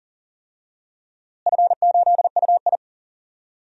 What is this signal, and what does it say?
Morse code, heard as a keyed tone, about 39 wpm, 690 Hz. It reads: F8UI
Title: DXCC-100 Call Signs - 4 Characters - Repeated 2x - 40wpm